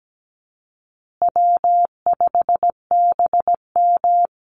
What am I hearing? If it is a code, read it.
Morse code, heard as a keyed tone, about 17 wpm, 700 Hz. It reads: W5BM
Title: Easy Call Signs Repeated 3x - 17wpm